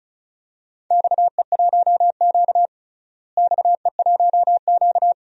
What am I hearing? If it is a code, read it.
Morse code, heard as a keyed tone, about 34 wpm, 700 Hz. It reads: XE1Q XE1Q
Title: DXCC-100 Call Signs - 4 Characters - Repeated 2x - 35wpm